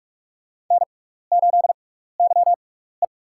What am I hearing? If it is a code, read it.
Morse code, heard as a keyed tone, about 42 wpm, spaced as for 15 wpm, 700 Hz. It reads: N8YE
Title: USA Call Signs - 6x Character Spacing - 45wpm